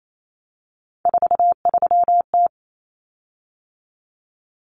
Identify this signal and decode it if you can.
Morse code, heard as a keyed tone, about 28 wpm, 700 Hz. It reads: 43T
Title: Common Cut Numbers - 3-Digit 28wpm